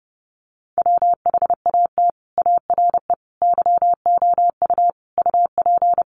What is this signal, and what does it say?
Morse code, heard as a keyed tone, about 30 wpm, 700 Hz. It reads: WHAT ARE YOU UP
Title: Sentences from Top 100 Words 30wpm